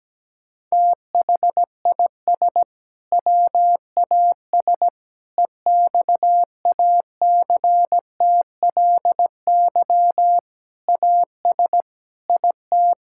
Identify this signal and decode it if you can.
Morse code, heard as a keyed tone, about 17 wpm, 700 Hz. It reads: THIS WAS EXACTLY AS IT
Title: Sentences from Top 1000 Words - Review Part 6 - 17wpm